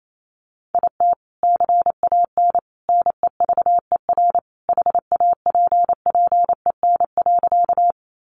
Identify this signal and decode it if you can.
Morse code, heard as a keyed tone, about 28 wpm, 700 Hz. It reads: IT CAN NEVER HAPPEN.
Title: Sentences from Top 400 Words 28wpm